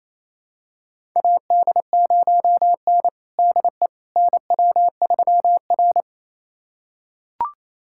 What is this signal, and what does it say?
Morse code, heard as a keyed tone, about 28 wpm, 700 Hz. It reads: AD0N DE NW3R
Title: USA Callsign DE Callsign - 4 Characters - 28wpm